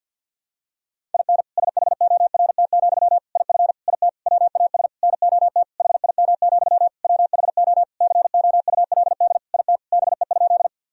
Morse code, heard as a keyed tone, about 47 wpm, 700 Hz. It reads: IN SHORT, IF IT WAS NOT HIM, WHO COULD IT BE?
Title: Sentences from Top 1000 Words - Review Part 2 - 50wpm